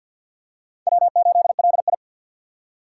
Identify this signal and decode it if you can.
Morse code, heard as a keyed tone, about 47 wpm, 700 Hz. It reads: W8LI
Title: USA Call Signs - 4 Characters - 3x Character Spacing - 50wpm